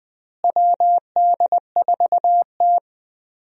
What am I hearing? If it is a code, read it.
Morse code, heard as a keyed tone, about 20 wpm, 700 Hz. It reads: WD4T
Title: USA Call Signs - 4 Characters - Repeated 3x - 20wpm